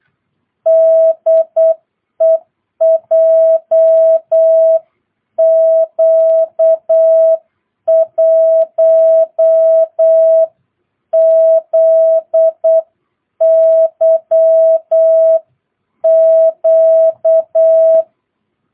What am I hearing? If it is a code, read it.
Morse code, heard as a keyed tone, about 8 wpm, 650 Hz. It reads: DE JQ1ZYQ
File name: jh8cbh / 2022cw